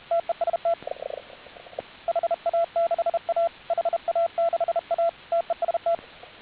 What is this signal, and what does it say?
Morse code, heard as a keyed tone, about 32 wpm, 1350 Hz, with radio noise. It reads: TEST HA6A HA6A TEST